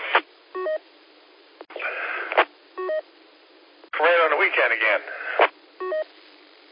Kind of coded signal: none